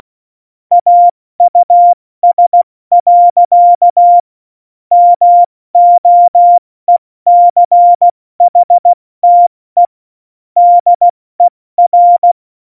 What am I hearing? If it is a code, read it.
Morse code, heard as a keyed tone, about 16 wpm, 700 Hz. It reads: AUS. MOECHTE DER
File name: Morsekurs / Witze / Witz 23. Juli 2019, 70 F80